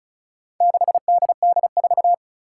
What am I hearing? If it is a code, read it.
Morse code, heard as a keyed tone, about 34 wpm, 700 Hz. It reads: 6DD4
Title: Four Challenging Letter-Number Combinations 35wpm